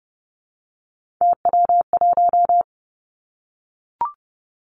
Morse code, heard as a keyed tone, about 30 wpm, 700 Hz. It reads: TW1
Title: Three Challenging Letter-Number Combinations 30wpm